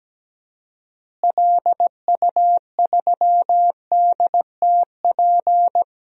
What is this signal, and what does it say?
Morse code, heard as a keyed tone, about 17 wpm, 700 Hz. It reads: LU3DTP
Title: DXCC-100 Call Signs - 6 Characters - Repeated 3x - 17wpm